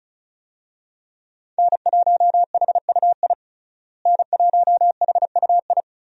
Morse code, heard as a keyed tone, about 36 wpm, 700 Hz. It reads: N1HUI N1HUI
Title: USA Call Signs - 5 Characters - Repeated 3x - 35wpm